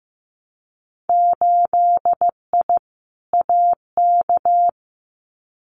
Morse code, heard as a keyed tone, about 15 wpm, 700 Hz. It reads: 8I AK
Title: Winter Field Day Exchanges Encore 15wpm